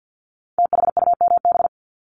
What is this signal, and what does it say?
Morse code, heard as a keyed tone, about 50 wpm, 700 Hz. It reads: THUMB
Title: Multi-letter Phonemes 50wpm